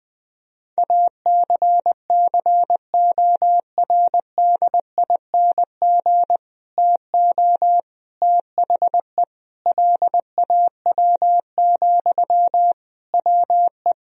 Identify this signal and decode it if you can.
Morse code, heard as a keyed tone, about 20 wpm, 700 Hz. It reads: ACCORDING TO THE LAW, WE
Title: Sentences from Top 1000 Words - Review Part 4 - 20wpm